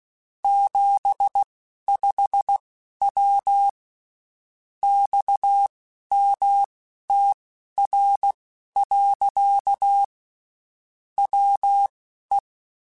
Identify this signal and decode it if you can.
Morse code, heard as a keyed tone, about 16 wpm, spaced as for 11 wpm, 800 Hz. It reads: 75W XMTR. WE
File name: (spaced) / QSOs Spaced-mp3s / WA7QOX 1 @15 sp